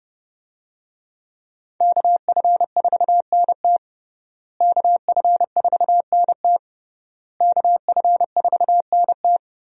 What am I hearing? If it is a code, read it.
Morse code, heard as a keyed tone, about 31 wpm, 700 Hz. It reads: KF4NT KF4NT KF4NT
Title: USA Call Signs - 5 Characters - Repeated 3x - 30wpm